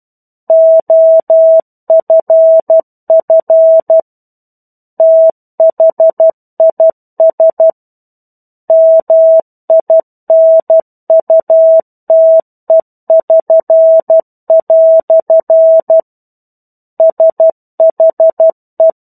Morse code, heard as a keyed tone, about 12 wpm, 650 Hz. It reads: OFF THIS MINUTE<SN>" SHE
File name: 01-Chapter-01 12 WPM